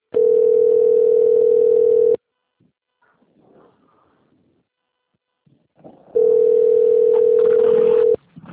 Ringback tone (North America), 440 + 480 Hz.
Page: Nova Iguacu - RJ